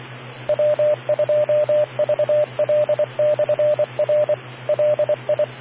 Morse code, heard as a keyed tone, about 24 wpm, 600 Hz, with radio noise. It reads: W2VL/R LI